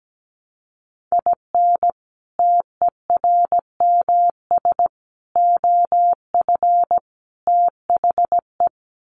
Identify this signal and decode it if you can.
Morse code, heard as a keyed tone, about 17 wpm, 700 Hz. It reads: IN TERMS OF THE
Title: Sets of 4 Words 17wpm